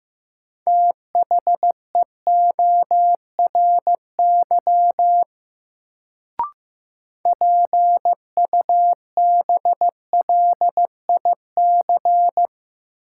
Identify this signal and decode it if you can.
Morse code, heard as a keyed tone, about 15 wpm, 700 Hz. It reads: THEORY PUBLIC
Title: Sentences from Top 600 Words - New Word Drill - 15wpm